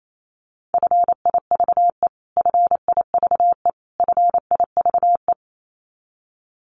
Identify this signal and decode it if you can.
Morse code, heard as a keyed tone, about 28 wpm, 700 Hz. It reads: FIVE FIVE FIVE
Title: Top 500 Words Repeated 28wpm